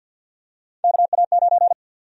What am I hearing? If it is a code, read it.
Morse code, heard as a keyed tone, about 50 wpm, 700 Hz. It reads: KA9
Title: USA Call Sign Prefixes - Repeated 2x - 50wpm